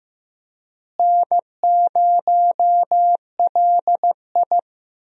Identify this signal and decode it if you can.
Morse code, heard as a keyed tone, about 15 wpm, 700 Hz. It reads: N0LI
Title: USA Call Signs - 4 Characters - Repeated 3x - 15wpm